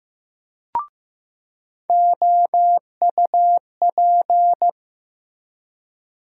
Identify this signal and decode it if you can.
Morse code, heard as a keyed tone, about 15 wpm, 700 Hz. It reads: OUP
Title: USA Call Sign Postfixes - 3 Letters 15wpm